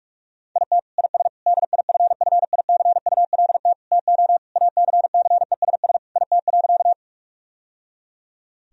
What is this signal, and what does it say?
Morse code, heard as a keyed tone, about 42 wpm, 700 Hz. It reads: IT IS DIFFICULT TO ACCESS IT.
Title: Sentences from Top 1000 Words - Review Part 4 - 45wpm